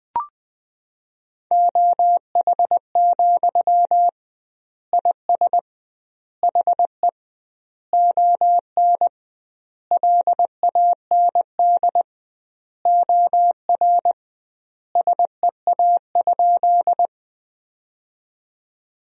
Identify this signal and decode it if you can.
Morse code, heard as a keyed tone, about 20 wpm, 700 Hz. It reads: OH, IS HE ON LAND OR SEA?
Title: 2x Word-Spacing - Sentences from Top 500 Words - 20wpm